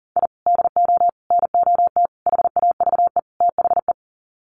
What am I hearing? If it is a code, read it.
Morse code, heard as a keyed tone, about 37 wpm, 700 Hz. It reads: I DO NOT HAVE THE
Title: Sets of 5 Words 40wpm